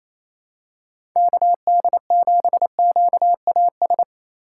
Morse code, heard as a keyed tone, about 28 wpm, 700 Hz. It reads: KD7QAS
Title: USA Call Signs - 6 Characters - 3x Character Spacing - 28wpm